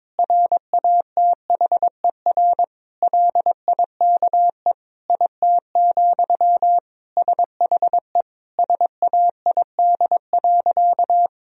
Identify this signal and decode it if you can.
Morse code, heard as a keyed tone, about 22 wpm, 700 Hz. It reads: RATHER LIKE IT, SHE SAID.
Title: Sentences from Top 1000 Words - Review Part 4 - 22wpm